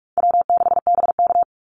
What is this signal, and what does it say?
Morse code, heard as a keyed tone, about 44 wpm, 700 Hz. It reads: W6BK
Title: Call Signs - 45wpm